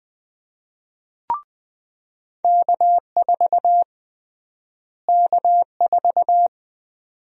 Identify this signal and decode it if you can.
Morse code, heard as a keyed tone, about 20 wpm, 700 Hz. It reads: K4 K4
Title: USA Call Sign Prefixes - Repeated 2x - 20wpm